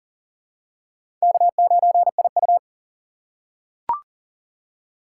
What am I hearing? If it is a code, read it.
Morse code, heard as a keyed tone, about 40 wpm, 700 Hz. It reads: K9IU
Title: USA Call Signs - 4 Characters - 2x Character Spacing - 40wpm